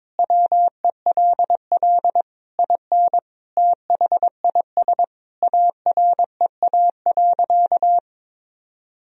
Morse code, heard as a keyed tone, about 22 wpm, 700 Hz. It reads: WELL IN THIS AREA.
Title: Sentences from Top 1000 Words - Review Part 2 - 22wpm